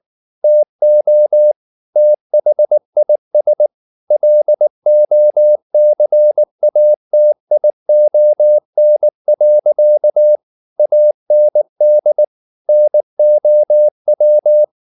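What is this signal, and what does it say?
Morse code, heard as a keyed tone, about 19 wpm, 600 Hz. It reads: TO THIS LOCATION. AND NOW
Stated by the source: Morse code at 19 wpm from CW med Gnister, reading Ken Follett's novel Never in English.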